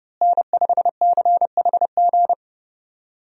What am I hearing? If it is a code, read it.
Morse code, heard as a keyed tone, about 31 wpm, 700 Hz. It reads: N5CHG
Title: USA Call Signs - 5 Characters - Repeated 2x - 30wpm